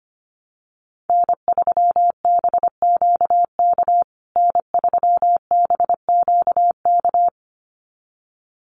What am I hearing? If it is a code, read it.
Morse code, heard as a keyed tone, about 25 wpm, 700 Hz. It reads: N3BQK N3BQK
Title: USA Call Signs Repeated 2x 25wpm